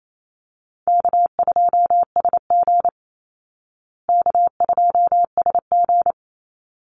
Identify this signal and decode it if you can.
Morse code, heard as a keyed tone, about 28 wpm, 700 Hz. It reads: K2SG K2SG
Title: Easy Call Signs Repeated 2x - 28wpm